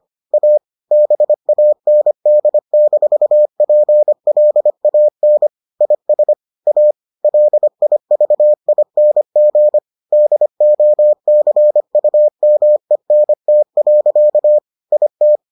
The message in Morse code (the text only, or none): A BAND-PLAN IS A LIVING DOCUMENT. IT